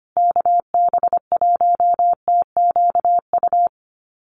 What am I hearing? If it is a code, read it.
Morse code, heard as a keyed tone, about 25 wpm, 700 Hz. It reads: KB1TQU
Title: USA Call Signs Repeated 2x 25wpm